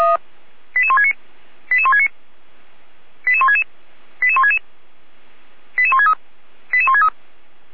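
Five-tone selective call (ZVEI), heard as quick runs of five tones, tones of about 70 ms. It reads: 88169 88169 88166 88166 88163 88163
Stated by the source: ZVEI five-tone call received by SDR, feuerwehr-orlamuende.org fire brigade server.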